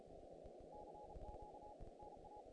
Morse code, heard as a keyed tone, about 33 wpm, 830 Hz, with noise. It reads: CQ NA
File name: contests / naqp-cw-k5tr-op-k5pi- -2100